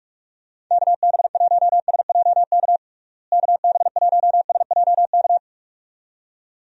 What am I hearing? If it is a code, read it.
Morse code, heard as a keyed tone, about 41 wpm, 700 Hz. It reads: KB1SJK KB1SJK
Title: USA Call Signs - 6 Characters - Repeated 2x - 45wpm